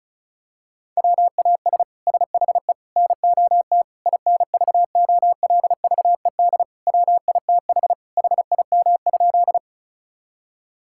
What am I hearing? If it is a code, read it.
Morse code, heard as a keyed tone, about 36 wpm, 700 Hz. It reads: WAS SHE NOT INVOLVED WITH HIM?
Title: Sentences from Top 1000 Words - Review Part 5 - 35wpm